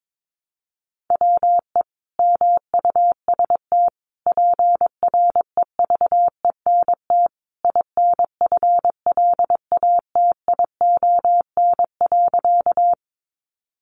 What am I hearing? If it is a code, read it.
Morse code, heard as a keyed tone, about 22 wpm, 700 Hz. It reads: WE MUST PREVENT INFLATION.
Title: Sentences from Top 1100 Words 22wpm